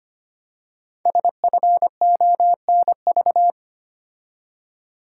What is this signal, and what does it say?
Morse code, heard as a keyed tone, about 25 wpm, 700 Hz. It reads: SFONV
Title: Code Groups - 5 Letters - 25wpm